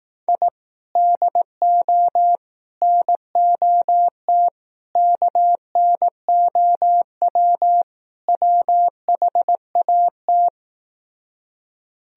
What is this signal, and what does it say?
Morse code, heard as a keyed tone, about 18 wpm, 700 Hz. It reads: I DO NOT KNOW WHAT
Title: Speed-Racing - Sets of 5 Words 15wpm